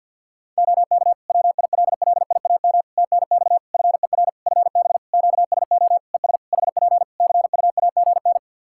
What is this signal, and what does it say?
Morse code, heard as a keyed tone, about 51 wpm, 700 Hz. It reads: OK WILLIAM TNX FER FB QSO ES HP CUAGN
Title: Practice QSO - Exchange 4 - Ending - 50wpm